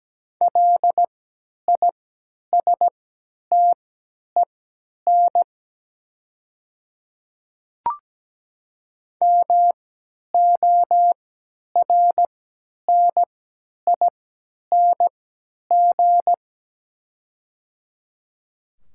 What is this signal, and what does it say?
Morse code, heard as a keyed tone, about 17 wpm, spaced as for 10 wpm, 700 Hz. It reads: LISTEN MORNING
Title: 3x Character Spacing - Top 500 Words - 17wpm